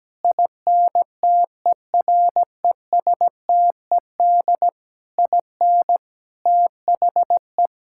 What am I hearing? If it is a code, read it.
Morse code, heard as a keyed tone, about 17 wpm, 700 Hz. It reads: INTERESTED IN THE
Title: Sentences from Top 1000 Words - Review Part 6 - 17wpm